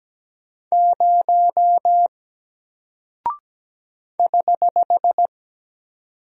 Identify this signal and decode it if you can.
Morse code, heard as a keyed tone, about 17 wpm, 700 Hz. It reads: 0 <HH>
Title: Single Character (Letter:Number:Punctuation:Prosign) Alt 17wpm